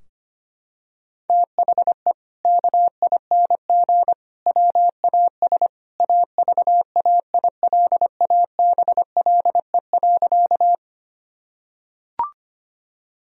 Morse code, heard as a keyed tone, about 25 wpm, 700 Hz. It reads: THE KING WAS AVAILABLE.
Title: Sentences from Top 1000 Words - Review Part 3 - 25wpm